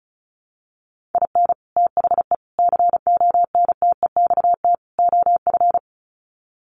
Morse code, heard as a keyed tone, about 34 wpm, 700 Hz. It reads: IN THE CONTEXT OF
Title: Sets of 4 Words 35wpm